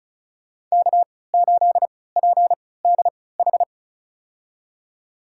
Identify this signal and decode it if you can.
Morse code, heard as a keyed tone, about 34 wpm, spaced as for 20 wpm, 700 Hz. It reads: K8PDH
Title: USA Call Signs - 3x Character Spacing - 35wpm